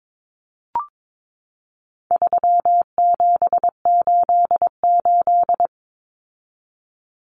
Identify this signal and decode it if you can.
Morse code, heard as a keyed tone, about 22 wpm, 700 Hz. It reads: 3788
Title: Common Cut Numbers - 4-Digit 22wpm